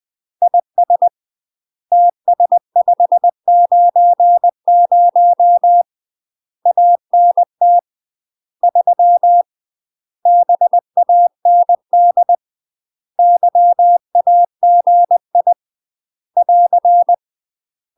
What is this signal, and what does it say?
Morse code, heard as a keyed tone, about 20 wpm, 700 Hz. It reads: IS TS590 ANT 3 BAND YAGI <AR>
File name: UK MASTER QSOs TRAINING / 20WPM / UK MASTER QSOs 0066